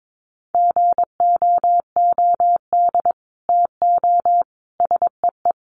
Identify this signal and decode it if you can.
Morse code, heard as a keyed tone, about 22 wpm, 700 Hz. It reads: GOOD TO SEE
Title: Sets of 4 Words 22wpm